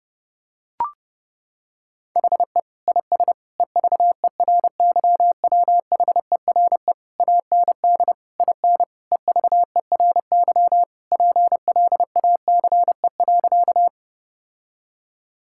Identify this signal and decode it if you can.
Morse code, heard as a keyed tone, about 30 wpm, 700 Hz. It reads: HE IS EVERYWHERE AND IN EVERY PLACE.
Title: Sentences from Top 1600 Words 30wpm